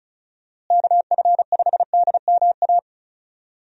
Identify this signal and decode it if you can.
Morse code, heard as a keyed tone, about 34 wpm, 700 Hz. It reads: KF5DMA
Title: USA Call Signs - 6 Characters - Repeated 3x - 35wpm